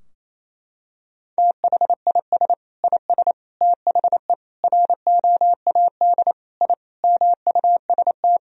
Morse code, heard as a keyed tone, about 28 wpm, 700 Hz. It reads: THIS IS THE ROAD I MUST
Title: Sentences from Top 1000 Words - Review Part 2 - 28wpm